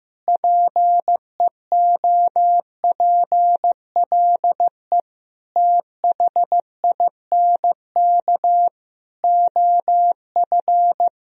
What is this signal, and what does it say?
Morse code, heard as a keyed tone, about 15 wpm, 700 Hz. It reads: PEOPLE THINK OF
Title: Sentences from Top 1000 Words - Review Part 3 - 15wpm